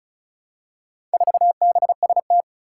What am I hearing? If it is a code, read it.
Morse code, heard as a keyed tone, about 36 wpm, 700 Hz. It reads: 4BST
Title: Four Challenging Letter-Number Combinations 35wpm